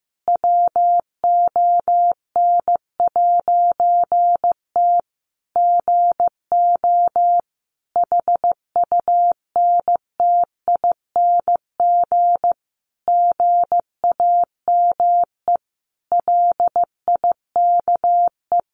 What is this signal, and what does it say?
Morse code, heard as a keyed tone, about 15 wpm, 700 Hz. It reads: WON'T GO HUNTING GAME LIKE
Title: A Sound Of Thunder 15wpm